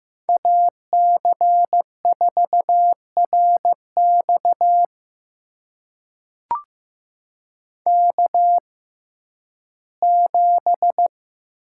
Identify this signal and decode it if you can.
Morse code, heard as a keyed tone, about 15 wpm, spaced as for 11 wpm, 700 Hz. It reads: AC4RX K7
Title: USA Call Signs - 5 Characters - 6x Character Spacing - 15wpm